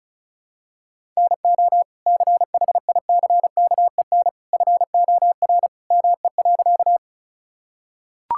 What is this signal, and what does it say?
Morse code, heard as a keyed tone, about 36 wpm, 700 Hz. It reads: NO CHICKEN FOR ME.
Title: Sentences from Top 1000 Words - Review Part 5 - 35wpm